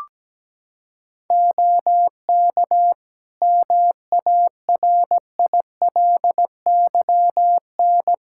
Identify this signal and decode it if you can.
Morse code, heard as a keyed tone, about 17 wpm, 700 Hz. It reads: OK MARILYN
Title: Practice QSO - Exchange 2 - Info - 17wpm